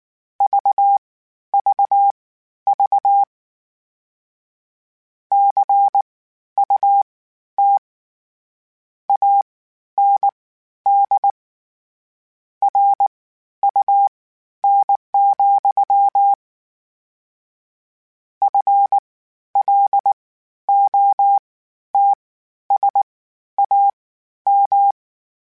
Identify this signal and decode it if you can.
Morse code, heard as a keyed tone, about 19 wpm, spaced as for 11 wpm, 800 Hz. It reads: VVV CUT AND RUN, FLOTSAM
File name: Misc / Misc-mp3s / WordPairs sp @18